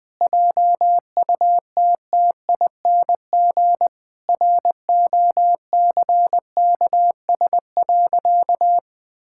Morse code, heard as a keyed tone, about 19 wpm, 700 Hz. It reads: JUTTING ROCKS.
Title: Pangram Challenge v2 - 20wpm